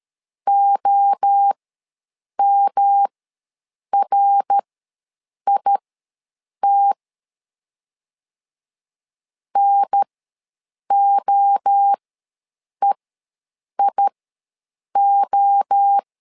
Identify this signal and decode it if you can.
Morse code, heard as a keyed tone, about 13 wpm, spaced as for 7 wpm, 800 Hz. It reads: OMRIT NOEIO
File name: LEZIONI / 3 MWRO / 3 MWRO ws 12wpm 1914